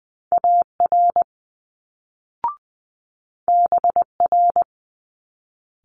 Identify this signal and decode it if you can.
Morse code, heard as a keyed tone, about 20 wpm, 700 Hz. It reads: AR BR
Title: Multi-letter Phonemes Quick Fire 20wpm